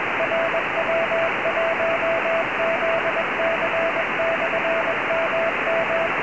Morse code, heard as a keyed tone, about 21 wpm, 660 Hz, with radio noise. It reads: RW1ZC/MM
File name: ei5fk / rw1zc ik18pq cq